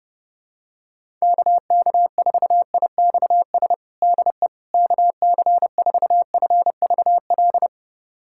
Morse code, heard as a keyed tone, about 32 wpm, 700 Hz. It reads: KK4IXS DE KC4FVL
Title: USA Callsign DE Callsign - 6 Characters - 30wpm